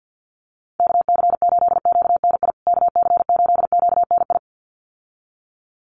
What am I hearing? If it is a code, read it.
Morse code, heard as a keyed tone, about 50 wpm, 700 Hz. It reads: KC8QNI KC8QNI
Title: USA Call Signs Repeated 2x 50wpm